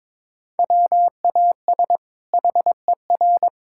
Morse code, heard as a keyed tone, about 22 wpm, 700 Hz. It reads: WAS HER
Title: Sentences from Top 1000 Words - Review Part 5 - 22wpm